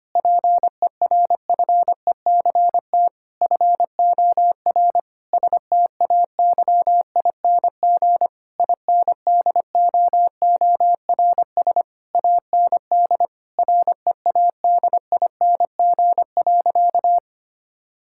Morse code, heard as a keyed tone, about 25 wpm, 700 Hz. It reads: PERFECT FOR STAYING INDOORS AND READING.
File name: practice / 25wpm / Difficult Sentences with Hints - 25wpm